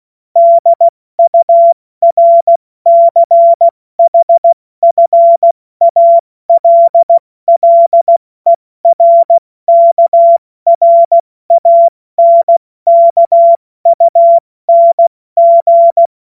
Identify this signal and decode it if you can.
Morse code, heard as a keyed tone, about 16 wpm, 680 Hz. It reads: DURCHFALLERKRANKUNG
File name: Morsekurs / Witze / Witz 01. Juni 2021, 70 F80